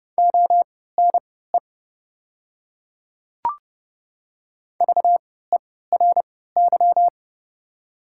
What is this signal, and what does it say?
Morse code, heard as a keyed tone, about 30 wpm, spaced as for 17 wpm, 690 Hz. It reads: ONE VERY
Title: 3x Character Spacing - Top 200 Words - 30wpm